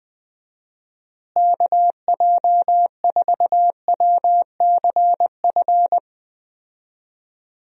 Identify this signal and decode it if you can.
Morse code, heard as a keyed tone, about 20 wpm, 700 Hz. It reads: KJ4WCF
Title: USA Call Signs - 6 Characters - Repeated 3x - 20wpm